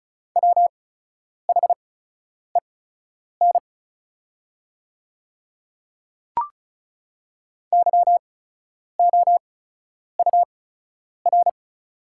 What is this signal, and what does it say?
Morse code, heard as a keyed tone, about 36 wpm, spaced as for 10 wpm, 700 Hz. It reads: WHEN YOUR
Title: 8x Character Spacing - Top 100 Words - 35wpm